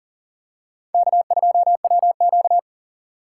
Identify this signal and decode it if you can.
Morse code, heard as a keyed tone, about 39 wpm, 690 Hz. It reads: K2WQ
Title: USA Call Signs - 4 Characters - 2x Character Spacing - 40wpm